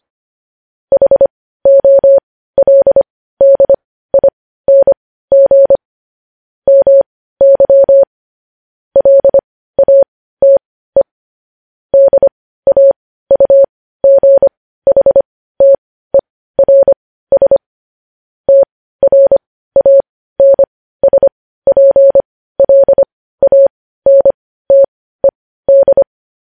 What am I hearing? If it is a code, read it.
Morse code, heard as a keyed tone, about 25 wpm, spaced as for 15 wpm, 550 Hz. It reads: HOLDING MY LATE DAUGHTERS TRANSPLANTED
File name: podcast / News headlines-1x-25x15